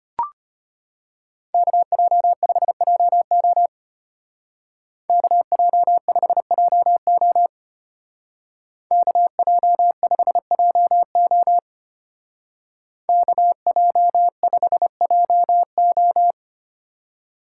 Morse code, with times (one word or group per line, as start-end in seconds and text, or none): 1.5-3.7 KJ5JO
5.1-7.5 KJ5JO
8.9-11.6 KJ5JO
13.1-16.3 KJ5JO